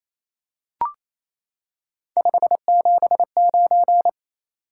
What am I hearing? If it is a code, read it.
Morse code, heard as a keyed tone, about 28 wpm, 700 Hz. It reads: 579
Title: Top 180 QSO Elements 28wpm